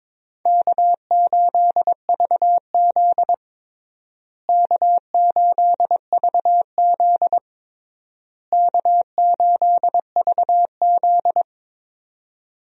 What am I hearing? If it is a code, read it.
Morse code, heard as a keyed tone, about 21 wpm, 700 Hz. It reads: K8VZ K8VZ K8VZ
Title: USA Call Signs - 4 Characters - Repeated 3x - 22wpm